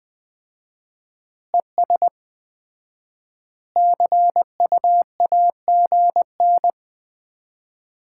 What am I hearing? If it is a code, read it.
Morse code, heard as a keyed tone, about 20 wpm, 700 Hz. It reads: ES CUAGN
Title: Common QSO Phrases - 4x Word-Spacing - 20wpm